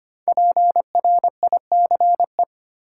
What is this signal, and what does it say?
Morse code, heard as a keyed tone, about 25 wpm, 700 Hz. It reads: PRICE
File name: 5 Letter Words - First 3 Letters Given - 25wpm